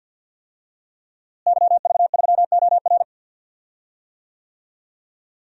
Morse code, heard as a keyed tone, about 48 wpm, 700 Hz. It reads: YV3OR
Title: DXCC-100 Call Signs - 5 Characters - Repeated 3x - 50wpm